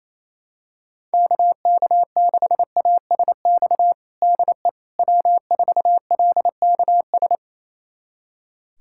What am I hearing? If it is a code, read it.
Morse code, heard as a keyed tone, about 28 wpm, 700 Hz. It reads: KK6ASX DE W4LKS
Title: USA Callsign DE Callsign 28wpm